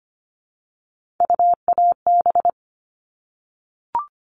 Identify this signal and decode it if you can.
Morse code, heard as a keyed tone, about 25 wpm, 700 Hz. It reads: UAB
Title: All Cut Numbers - 3-Digit 25wpm